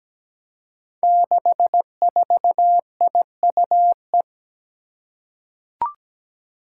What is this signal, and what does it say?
Morse code, heard as a keyed tone, about 17 wpm, 700 Hz. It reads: 64IUE
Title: Five Challenging Letter-Number Combinations 17wpm